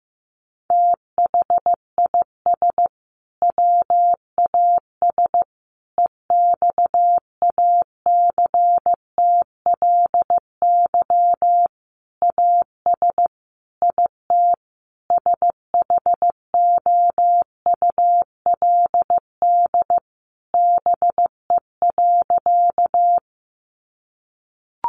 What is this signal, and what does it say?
Morse code, heard as a keyed tone, about 15 wpm, 700 Hz. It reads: THIS WAS EXACTLY AS IT SHOULD BE.
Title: Sentences from Top 900 Words 15wpm